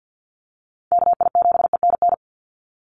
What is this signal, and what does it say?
Morse code, heard as a keyed tone, about 48 wpm, 700 Hz. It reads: KI7ENN
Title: USA Call Signs 50wpm